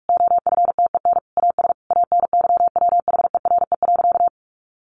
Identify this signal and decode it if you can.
Morse code, heard as a keyed tone, about 44 wpm, 700 Hz. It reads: OFTEN AS ANYWHERE.
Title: Sentences from Top 1400 Words 45wpm